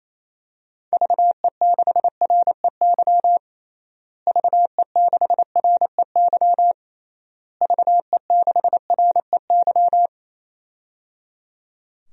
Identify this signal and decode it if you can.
Morse code, heard as a keyed tone, about 28 wpm, 700 Hz. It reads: VE6REY VE6REY VE6REY
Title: DXCC-100 Call Signs - 6 Characters - Repeated 3x - 28wpm